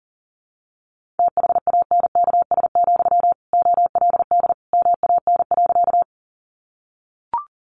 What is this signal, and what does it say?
Morse code, heard as a keyed tone, about 40 wpm, 700 Hz. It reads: THANKS, OLD MAN.
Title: Sentences from Top 600 Words 40wpm